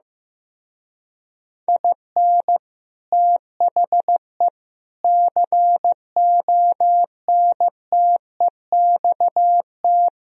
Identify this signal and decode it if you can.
Morse code, heard as a keyed tone, about 15 wpm, 700 Hz. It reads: IN THE CONTEXT
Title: Speed-Racing - Sets of 4 Words 15wpm